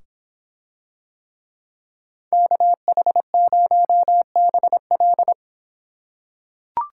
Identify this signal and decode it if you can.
Morse code, heard as a keyed tone, about 25 wpm, 700 Hz. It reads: KH0BL